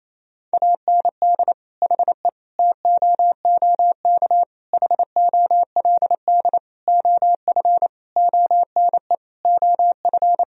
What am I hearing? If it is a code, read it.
Morse code, heard as a keyed tone, about 28 wpm, 700 Hz. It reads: AND HE TOOK HOLD OF ONE OF
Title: Sentences from Top 1000 Words - Review Part 4 - 28wpm